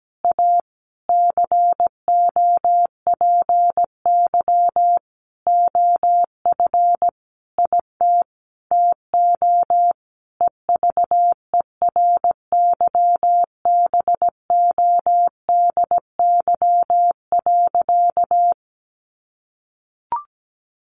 Morse code, heard as a keyed tone, about 17 wpm, 700 Hz. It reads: A COPY OF IT TO EVERYBODY.
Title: Sentences from Top 1400 Words 17wpm